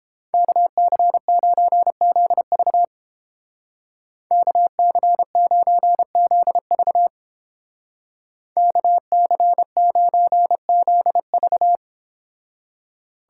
Morse code, heard as a keyed tone, about 29 wpm, 700 Hz. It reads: KC9ZV KC9ZV KC9ZV